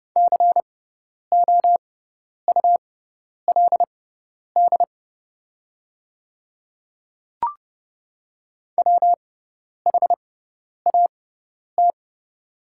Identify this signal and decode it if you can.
Morse code, heard as a keyed tone, about 31 wpm, spaced as for 10 wpm, 700 Hz. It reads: COULD WHAT
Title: 6x Character Spacing - Top 100 Words - 30wpm